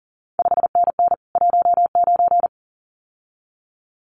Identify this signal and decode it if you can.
Morse code, heard as a keyed tone, about 41 wpm, 700 Hz. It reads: 5NN 19
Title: CQ WW DX Contest Exchange Encore - 40wpm